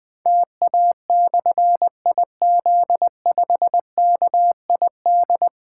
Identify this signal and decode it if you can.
Morse code, heard as a keyed tone, about 20 wpm, 700 Hz. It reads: TA/IZ5KID
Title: Difficult Call Signs 20wpm